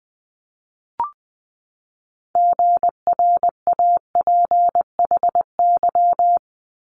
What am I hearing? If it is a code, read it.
Morse code, heard as a keyed tone, about 20 wpm, 700 Hz. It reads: GRAPHY
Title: Multi-letter Phonemes 20wpm